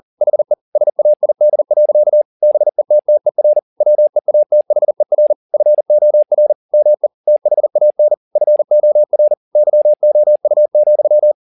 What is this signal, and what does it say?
Morse code, heard as a keyed tone, about 40 wpm, 600 Hz. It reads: HE SAID. BETTER WEATHER FOR ME THAN FOR YOU,